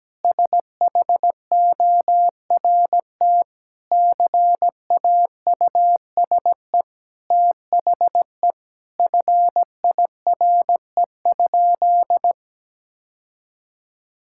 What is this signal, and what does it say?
Morse code, heard as a keyed tone, about 17 wpm, 700 Hz. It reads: SHORT CAUSE THE FIRE?
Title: Sentences from Top 1000 Words - Review Part 6 - 17wpm